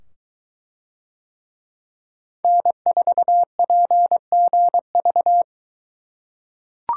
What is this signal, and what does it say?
Morse code, heard as a keyed tone, about 23 wpm, 700 Hz. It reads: N4PGV